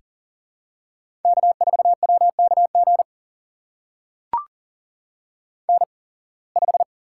Morse code, heard as a keyed tone, about 40 wpm, spaced as for 26 wpm, 700 Hz. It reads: K4WKG N5
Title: USA Call Signs - 5 Characters - 8x Character Spacing - 40wpm